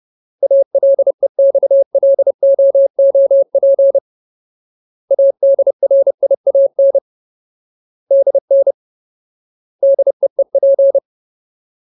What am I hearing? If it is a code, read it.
Morse code, heard as a keyed tone, about 30 wpm, 550 Hz. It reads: ALEXLOOP ADRIAN DN DEEP
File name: Ultimate Word List / 30WPM Ultimate